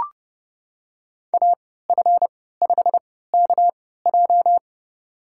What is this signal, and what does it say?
Morse code, heard as a keyed tone, about 30 wpm, spaced as for 17 wpm, 700 Hz. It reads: AF5KJ
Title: USA Call Signs - 3x Character Spacing - 30wpm